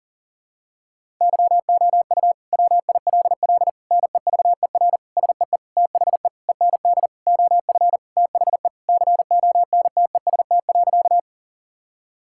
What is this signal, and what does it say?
Morse code, heard as a keyed tone, about 40 wpm, 700 Hz. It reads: YOU WILL NEVER SEE THE END OF THE CONTEST.
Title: Sentences from Top 1600 Words 40wpm